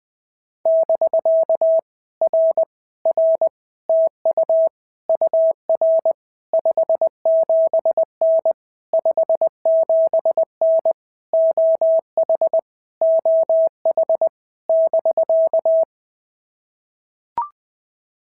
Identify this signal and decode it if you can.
Morse code, heard as a keyed tone, about 20 wpm, 650 Hz. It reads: <BK> R R TU UR 57N 57N OH OH <BK>
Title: POTA QSO Practice - 20wpm